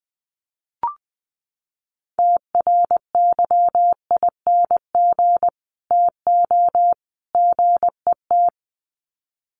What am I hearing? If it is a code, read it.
Morse code, heard as a keyed tone, about 20 wpm, 700 Hz. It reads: TRYING TO GET
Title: Sets of 3 Words 20wpm